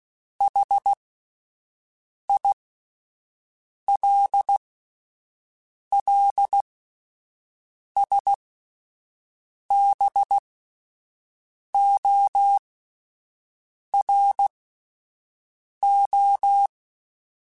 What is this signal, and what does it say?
Morse code, heard as a keyed tone, about 16 wpm, spaced as for 5 wpm, 800 Hz. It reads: HILLSBORO